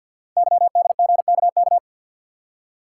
Morse code, heard as a keyed tone, about 48 wpm, 700 Hz. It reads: YDGKK
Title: Code Groups - 5 Letters - 50wpm